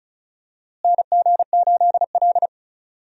Morse code, heard as a keyed tone, about 36 wpm, 700 Hz. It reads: NG8L
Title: USA Call Signs - 4 Characters - Repeated 3x - 35wpm